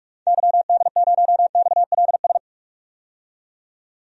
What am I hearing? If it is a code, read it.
Morse code, heard as a keyed tone, about 42 wpm, 700 Hz. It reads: YD0XLS
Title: DXCC-100 Call Signs - 6 Characters - 45wpm